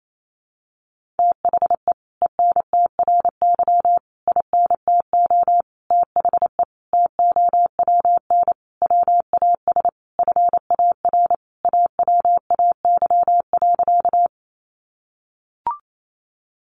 Morse code, heard as a keyed tone, about 28 wpm, 700 Hz. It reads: THE ENTRY INTO THE TOWN WAS FAR AWAY.
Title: Sentences from Top 1500 Words 28wpm